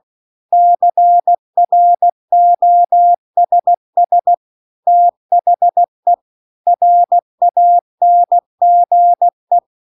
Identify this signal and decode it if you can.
Morse code, heard as a keyed tone, about 16 wpm, 700 Hz. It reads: CROSS THE RANGE